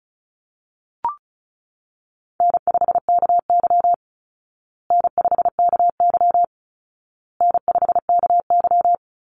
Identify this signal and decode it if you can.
Morse code, heard as a keyed tone, about 34 wpm, 700 Hz. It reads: N5KY N5KY N5KY
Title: Easy Call Signs Repeated 3x - 35wpm